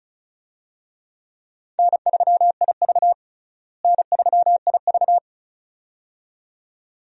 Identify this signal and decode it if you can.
Morse code, heard as a keyed tone, about 34 wpm, 700 Hz. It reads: N3IV N3IV
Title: USA Call Signs - 4 Characters - Repeated 2x - 35wpm